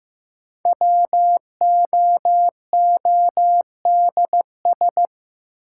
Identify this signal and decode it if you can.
Morse code, heard as a keyed tone, about 15 wpm, 700 Hz. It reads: WOODS
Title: Sentences from Top 1000 Words - Review Part 2 - New Word Drill - 15wpm